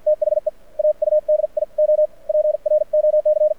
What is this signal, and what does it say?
Morse code, heard as a keyed tone, about 53 wpm, spaced as for 45 wpm, 610 Hz. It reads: THE AUDIO PROC
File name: archived / CRSnet